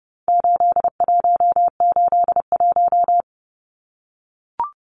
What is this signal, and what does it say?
Morse code, heard as a keyed tone, about 30 wpm, 700 Hz. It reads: 8181
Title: Numbers - 4-Digit 30wpm